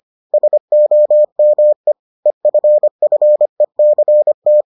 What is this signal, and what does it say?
Morse code, heard as a keyed tone, about 25 wpm, 600 Hz. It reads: SOME EFFECT